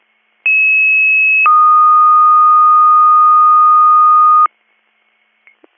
Two-tone page, heard as a one-second tone, then a three-second tone, about 2600 then 1230 Hz. It is heard from 0.5 to 4.5 s.